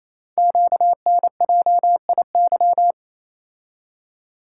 Morse code, heard as a keyed tone, about 27 wpm, 700 Hz. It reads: QNJIY
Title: Code Groups - 5 Letters - 28wpm